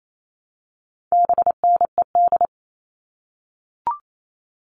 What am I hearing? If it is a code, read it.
Morse code, heard as a keyed tone, about 28 wpm, 700 Hz. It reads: BNED
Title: All Cut Numbers - 4-Digit 28wpm